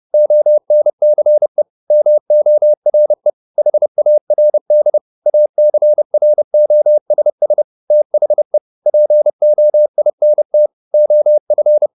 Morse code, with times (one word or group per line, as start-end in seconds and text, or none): 0.1-1.6 ONCE
1.9-3.3 MORE
3.6-5.0 HARD
5.3-7.6 ACROSS
7.9-8.6 THE
8.9-10.7 POINT
10.9-11.9 OF